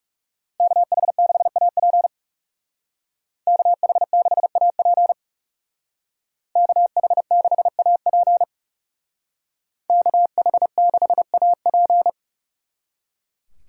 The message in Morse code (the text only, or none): KH6AP KH6AP KH6AP KH6AP